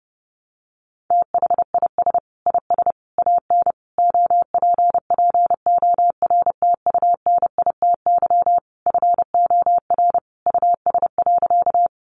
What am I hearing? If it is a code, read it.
Morse code, heard as a keyed tone, about 30 wpm, 700 Hz. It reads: THIS IS AN OPPORTUNITY FOR US.
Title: Sentences from Top 1000 Words 30wpm